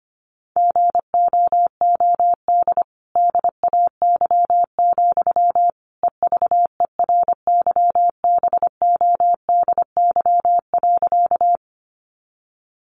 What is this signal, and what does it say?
Morse code, heard as a keyed tone, about 25 wpm, 700 Hz. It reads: GOOD DAY, EVERYBODY.
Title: Sentences from Top 1400 Words 25wpm